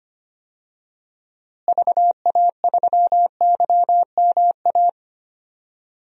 Morse code, heard as a keyed tone, about 25 wpm, 700 Hz. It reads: VA3YMA
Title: DXCC-100 Call Signs - 6 Characters - Repeated 2x - 25wpm